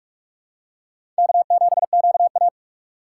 Morse code, heard as a keyed tone, about 45 wpm, 700 Hz. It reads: K7QA
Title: USA Call Signs - 2x Character Spacing - 45wpm